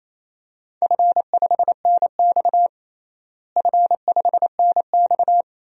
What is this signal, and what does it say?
Morse code, heard as a keyed tone, about 28 wpm, 700 Hz. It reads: F5NX F5NX
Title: DXCC-100 Call Signs - 4 Characters - Repeated 2x - 28wpm